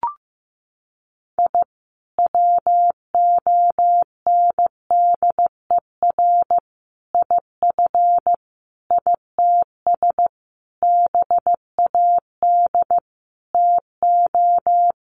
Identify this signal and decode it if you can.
Morse code, heard as a keyed tone, about 15 wpm, 700 Hz. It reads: I WONDER IF ITS BAD TO
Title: Sentences from Top 700 Words 15wpm